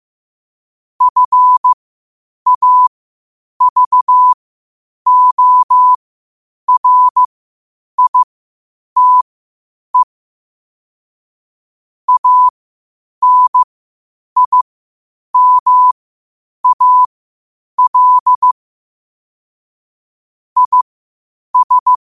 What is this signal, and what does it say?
Morse code, heard as a keyed tone, about 15 wpm, spaced as for 8 wpm, 1000 Hz. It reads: FAVORITE ANIMAL IS
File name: qotdfiles / 8wpm